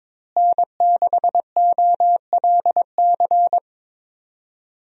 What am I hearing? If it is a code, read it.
Morse code, heard as a keyed tone, about 22 wpm, 700 Hz. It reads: N6OLC
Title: USA Call Signs - 5 Characters - Repeated 2x - 22wpm